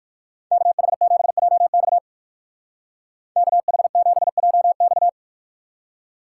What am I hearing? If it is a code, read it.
Morse code, heard as a keyed tone, about 59 wpm, 700 Hz. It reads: KH7JX KH7JX